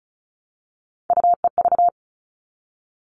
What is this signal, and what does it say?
Morse code, heard as a keyed tone, about 34 wpm, 700 Hz. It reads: UEV
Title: All Cut Numbers - 3-Digit 35wpm